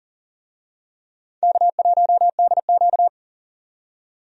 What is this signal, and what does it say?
Morse code, heard as a keyed tone, about 40 wpm, 700 Hz. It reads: K1DQ
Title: USA Call Signs - 4 Characters - 40wpm